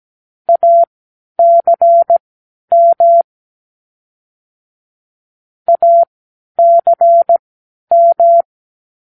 Morse code, heard as a keyed tone, about 17 wpm, spaced as for 10 wpm, 680 Hz. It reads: ACM ACM
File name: k7qo / Practicewords